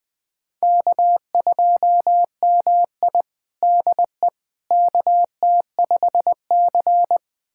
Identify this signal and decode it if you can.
Morse code, heard as a keyed tone, about 20 wpm, 700 Hz. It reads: K2MI DE KT5C
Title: USA Callsign DE Callsign - 4 Characters - 20wpm